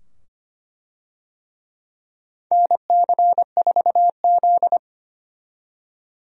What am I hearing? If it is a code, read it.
Morse code, heard as a keyed tone, about 25 wpm, 700 Hz. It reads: NC4Z
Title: USA Call Signs - 4 Characters - 25wpm